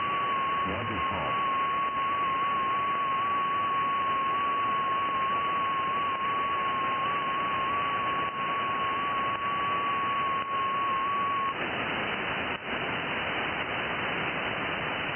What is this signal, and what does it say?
FT8.
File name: IC-7100